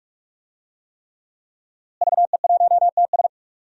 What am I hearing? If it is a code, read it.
Morse code, heard as a keyed tone, about 43 wpm, 700 Hz. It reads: VE1TS
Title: DXCC-100 Call Signs - 5 Characters - Repeated 3x - 45wpm